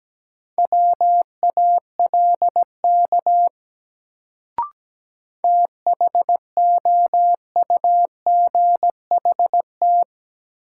Morse code, heard as a keyed tone, about 17 wpm, 700 Hz. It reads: WALK THOUGHT
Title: Sentences from Top 300 Words - New Word Drill - 17wpm